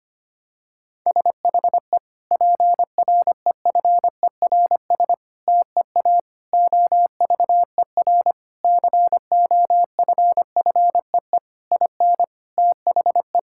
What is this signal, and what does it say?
Morse code, heard as a keyed tone, about 25 wpm, 700 Hz. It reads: SHE PREFERS TEA OVER COFFEE IN THE
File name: practice / 25wpm / Difficult Sentences with Hints - 25wpm